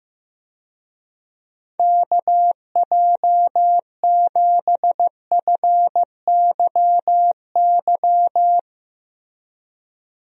Morse code, heard as a keyed tone, about 15 wpm, 700 Hz. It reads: KJ7FYY
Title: USA Call Signs - 6 Characters - Repeated 3x - 15wpm